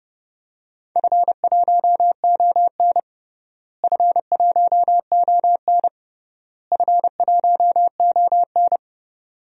Morse code, heard as a keyed tone, about 31 wpm, 700 Hz. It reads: F1ON F1ON F1ON
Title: DXCC-100 Call Signs - 4 Characters - Repeated 3x - 30wpm